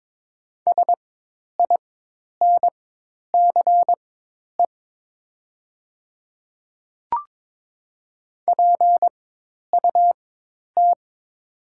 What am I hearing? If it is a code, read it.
Morse code, heard as a keyed tone, about 22 wpm, spaced as for 10 wpm, 700 Hz. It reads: SINCE PUT
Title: 4x Character Spacing - Top 500 Words - 22wpm